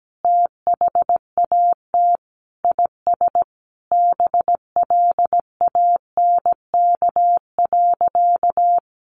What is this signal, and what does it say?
Morse code, heard as a keyed tone, about 17 wpm, 700 Hz. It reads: THAT IS BLANK.
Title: Sentences from Top 1500 Words 17wpm